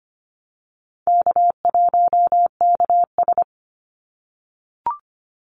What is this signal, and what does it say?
Morse code, heard as a keyed tone, about 25 wpm, 700 Hz. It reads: K1KS
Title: Easy Call Signs - 25wpm